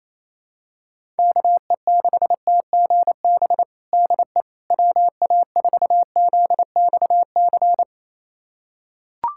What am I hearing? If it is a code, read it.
Morse code, heard as a keyed tone, about 28 wpm, 700 Hz. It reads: KE6TGB DE WA4ZXC
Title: USA Callsign DE Callsign 28wpm